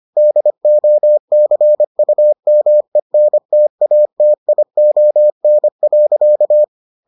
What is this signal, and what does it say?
Morse code, heard as a keyed tone, about 25 wpm, 600 Hz. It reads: DOCUMENTATION.